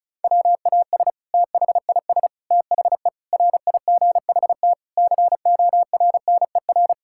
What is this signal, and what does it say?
Morse code, heard as a keyed tone, about 34 wpm, 700 Hz. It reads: WAS THIS THE RIGHT CORNER
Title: Sentences from Top 1600 Words 35wpm